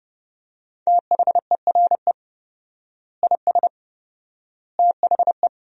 Morse code, heard as a keyed tone, about 31 wpm, 700 Hz. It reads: THERE IS THE
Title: 4x Word-Spacing - Sentences from Top 200 Words - 30wpm